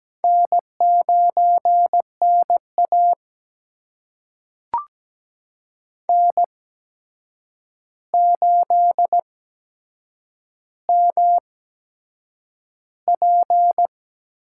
Morse code, heard as a keyed tone, about 17 wpm, spaced as for 7 wpm, 700 Hz. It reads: N9NA N8MP
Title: USA Call Signs - 4 Characters - 8x Character Spacing - 17wpm